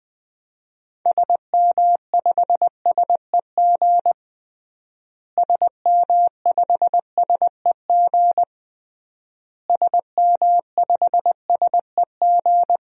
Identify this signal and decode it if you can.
Morse code, heard as a keyed tone, about 20 wpm, 700 Hz. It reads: SM5SEG SM5SEG SM5SEG
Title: DXCC-100 Call Signs - 6 Characters - Repeated 3x - 20wpm